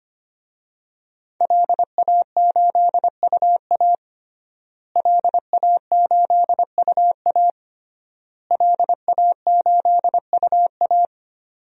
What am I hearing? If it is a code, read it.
Morse code, heard as a keyed tone, about 25 wpm, 700 Hz. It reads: LA8UA LA8UA LA8UA
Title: DXCC-100 Call Signs - 5 Characters - Repeated 3x - 25wpm